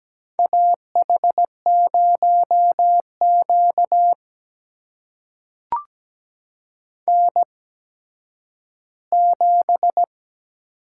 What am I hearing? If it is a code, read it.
Morse code, heard as a keyed tone, about 17 wpm, spaced as for 10 wpm, 700 Hz. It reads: AH0Q N7
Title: USA Call Signs - 4 Characters - 8x Character Spacing - 17wpm